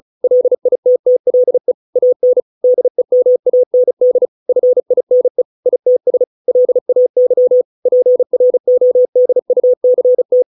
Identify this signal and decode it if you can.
Morse code, heard as a keyed tone, about 34 wpm, 500 Hz. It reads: LITTLE AN DEMAND FINE ITS LAY PRODUCT